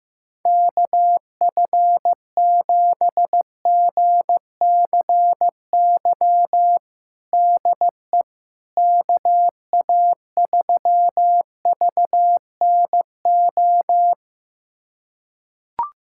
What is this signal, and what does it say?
Morse code, heard as a keyed tone, about 15 wpm, 700 Hz. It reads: KF7GCY DE KA3VNO
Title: USA Callsign DE Callsign - 6 Characters - 15wpm